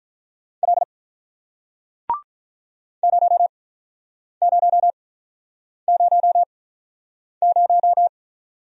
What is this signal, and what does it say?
Morse code, heard as a keyed tone, about 48 wpm, spaced as for 9 wpm, 700 Hz. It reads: L 0000